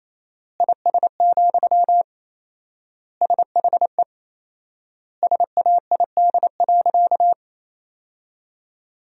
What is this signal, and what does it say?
Morse code, heard as a keyed tone, about 28 wpm, 700 Hz. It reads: IS, SHE SAID.
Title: 4x Word-Spacing - Sentences from Top 100 Words - 28wpm